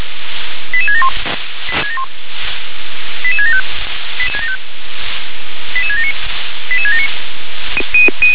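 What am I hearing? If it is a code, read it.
Five-tone selective call (ZVEI), heard as quick runs of five tones, tones of about 70 ms. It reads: 88671 88675 88679 88679
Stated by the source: ZVEI five-tone selective call received by SDR.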